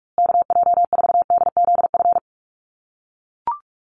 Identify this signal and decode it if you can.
Morse code, heard as a keyed tone, about 44 wpm, 700 Hz. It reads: KJ4DZF
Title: USA Call Signs Repeated 2x 45wpm